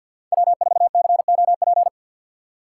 Morse code, heard as a keyed tone, about 52 wpm, 700 Hz. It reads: W4COP
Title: USA Call Signs - 5 Characters - 6x Character Spacing - 50wpm